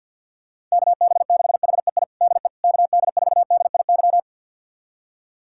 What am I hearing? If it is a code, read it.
Morse code, heard as a keyed tone, about 51 wpm, 700 Hz. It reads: KB6HEI DE KD4DIY
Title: USA Callsign DE Callsign - 6 Characters - 50wpm